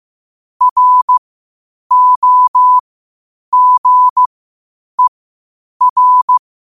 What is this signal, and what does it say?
Morse code, heard as a keyed tone, about 15 wpm, spaced as for 8 wpm, 1000 Hz. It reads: ROGER
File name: qotdfiles / 8wpm